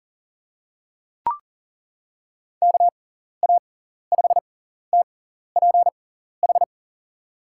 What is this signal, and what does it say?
Morse code, heard as a keyed tone, about 37 wpm, spaced as for 14 wpm, 700 Hz. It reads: KA5TPH
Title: USA Call Signs - 6 Characters - 6x Character Spacing - 40wpm